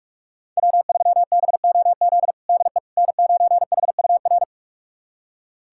Morse code, heard as a keyed tone, about 46 wpm, 700 Hz. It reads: W3BOZ DE N9HUR
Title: USA Callsign DE Callsign - 5 Characters - 45wpm